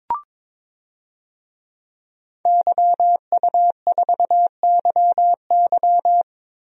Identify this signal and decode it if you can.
Morse code, heard as a keyed tone, about 22 wpm, 700 Hz. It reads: YU4YY
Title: DXCC-100 Call Signs - 5 Characters - Repeated 3x - 22wpm